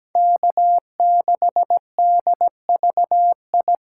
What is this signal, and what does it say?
Morse code, heard as a keyed tone, about 17 wpm, 700 Hz. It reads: K6DVI
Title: USA Call Signs - 5 Characters - 2x Character Spacing - 17wpm